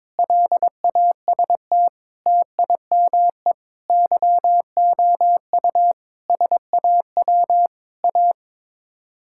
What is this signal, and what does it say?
Morse code, heard as a keyed tone, about 22 wpm, 700 Hz. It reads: LAST TIME YOU SAW A
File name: practice / 22wpm / Sets of 8 Words - 22wpm